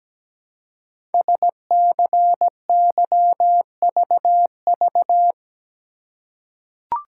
Morse code, heard as a keyed tone, about 17 wpm, 700 Hz. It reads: SCYVV
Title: Code Groups - 5 Letters - 17wpm